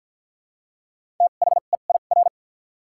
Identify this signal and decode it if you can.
Morse code, heard as a keyed tone, about 48 wpm, spaced as for 35 wpm, 700 Hz. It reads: THEIR
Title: 2x Character Spacing - Top 100 Words - 50wpm